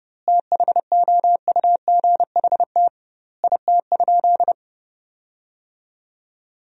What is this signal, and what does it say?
Morse code, heard as a keyed tone, about 30 wpm, 700 Hz. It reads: THOUGHT IT?
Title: 2x Word-Spacing - Sentences from Top 300 Words - 30wpm